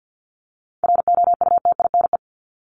Morse code, heard as a keyed tone, about 50 wpm, 700 Hz. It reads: ROUTINE
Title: Top 1501-1750 Words 50wpm